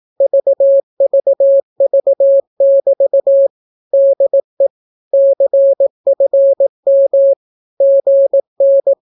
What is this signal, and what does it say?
Morse code, heard as a keyed tone, about 18 wpm, 550 Hz. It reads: VVV<BT> DE CFM GN
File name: naukapodstaw / skroty / skroty